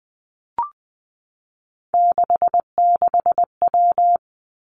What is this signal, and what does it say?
Morse code, heard as a keyed tone, about 20 wpm, 700 Hz. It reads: 66W
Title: Three Challenging Letter-Number Combinations 20wpm